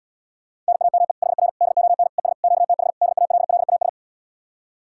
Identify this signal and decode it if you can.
Morse code, heard as a keyed tone, about 77 wpm, 700 Hz. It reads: NAME HR DON ES QTH DALLAS